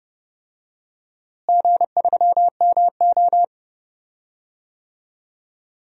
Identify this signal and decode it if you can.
Morse code, heard as a keyed tone, about 32 wpm, 700 Hz. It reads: G3MO
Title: DXCC-100 Call Signs - 4 Characters - Repeated 3x - 30wpm